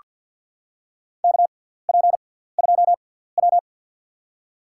Morse code, heard as a keyed tone, about 47 wpm, spaced as for 17 wpm, 700 Hz. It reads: KP2W
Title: USA Call Signs - 6x Character Spacing - 50wpm